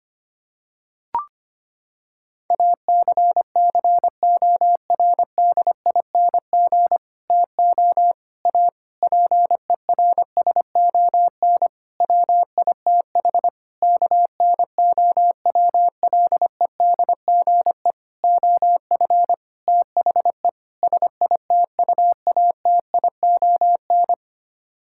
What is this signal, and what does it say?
Morse code, heard as a keyed tone, about 25 wpm, 700 Hz. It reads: ACCORDING TO A PERSON WITH KNOWLEDGE OF THE SITUATION
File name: Sets of 9 Words - 25wpm